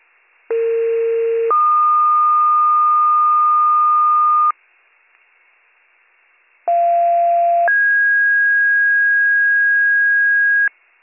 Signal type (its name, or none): two-tone page